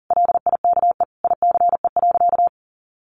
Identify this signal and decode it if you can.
Morse code, heard as a keyed tone, about 37 wpm, 700 Hz. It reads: LIKE ICE.
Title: Sentences from Top 1000 Words 40wpm